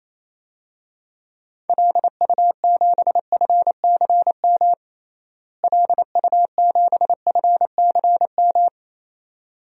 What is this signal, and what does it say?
Morse code, heard as a keyed tone, about 27 wpm, 700 Hz. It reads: LU7FCM LU7FCM
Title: DXCC-100 Call Signs - 6 Characters - Repeated 2x - 28wpm